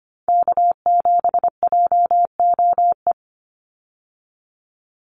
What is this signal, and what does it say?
Morse code, heard as a keyed tone, about 25 wpm, 700 Hz. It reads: K7JOE
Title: Call Signs - 25wpm